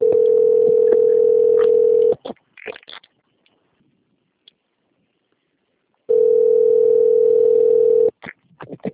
Ringback tone (North America), 440 + 480 Hz.